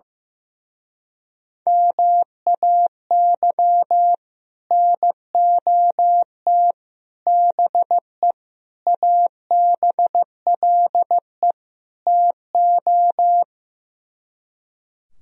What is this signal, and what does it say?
Morse code, heard as a keyed tone, about 16 wpm, 700 Hz. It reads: MAY NOT BE ABLE TO
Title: Speed-Racing - Sets of 5 Words 15wpm